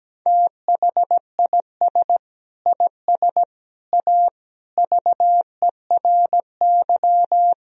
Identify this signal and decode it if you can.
Morse code, heard as a keyed tone, about 17 wpm, 700 Hz. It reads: THIS IS A VERY
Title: Sentences from Top 1000 Words - Review Part 3 - 17wpm